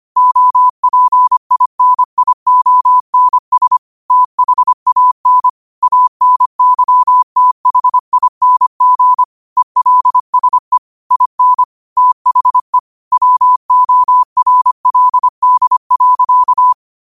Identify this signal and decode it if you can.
Morse code, heard as a keyed tone, about 25 wpm, 1000 Hz. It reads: OPINIONS THAN ANYTHING ELSE IN THE WORLD.
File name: qotdfiles / 25wpm